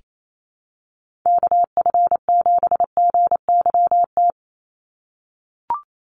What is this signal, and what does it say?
Morse code, heard as a keyed tone, about 28 wpm, 700 Hz. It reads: KF7GYT
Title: USA Call Signs 28wpm